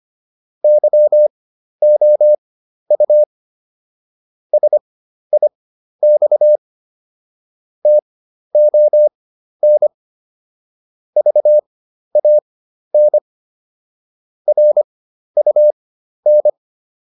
Morse code, heard as a keyed tone, about 25 wpm, spaced as for 12 wpm, 600 Hz. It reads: YOU SIX TON VAN RUN